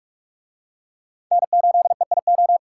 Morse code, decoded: N8EIO

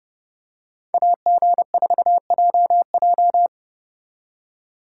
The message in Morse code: AG4JJ

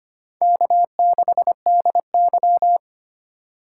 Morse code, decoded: K6DY

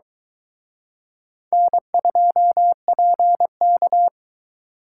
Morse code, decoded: N2PK

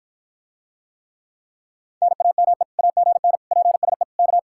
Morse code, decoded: NAME AGN PSE K

